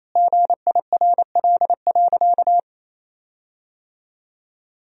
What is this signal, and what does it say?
Morse code, heard as a keyed tone, about 28 wpm, 700 Hz. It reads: GIRL.
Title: 4x Word-Spacing - Sentences from Top 400 Words - 28wpm